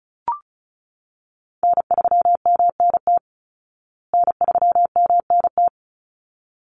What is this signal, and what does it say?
Morse code, heard as a keyed tone, about 34 wpm, 700 Hz. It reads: N3MNT N3MNT
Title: Call Signs Repeated 3x - 35wpm